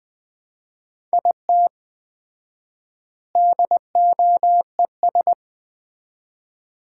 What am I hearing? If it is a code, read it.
Morse code, heard as a keyed tone, about 20 wpm, 700 Hz. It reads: IT DOES
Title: 4x Word-Spacing - Sets of 3 Words - 20wpm